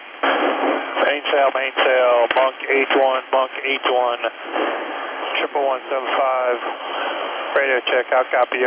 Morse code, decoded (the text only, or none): none